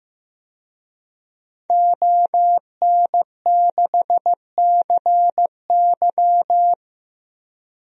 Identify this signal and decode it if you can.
Morse code, heard as a keyed tone, about 15 wpm, 700 Hz. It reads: ON6CY
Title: DXCC-100 Call Signs - 5 Characters - 15wpm